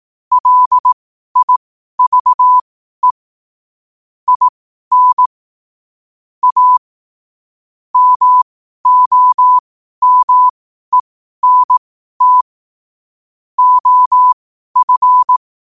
Morse code, heard as a keyed tone, about 18 wpm, spaced as for 12 wpm, 1000 Hz. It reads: LIVE IN A MOMENT OF
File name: qotdfiles / 12wpm